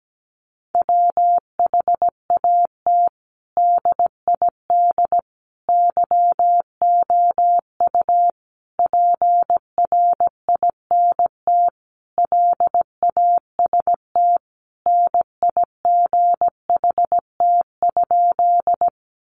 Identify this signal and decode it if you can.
Morse code, heard as a keyed tone, about 17 wpm, 700 Hz. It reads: WHAT DID YOU PRINT LAST NIGHT?
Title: Sentences from Top 1400 Words 17wpm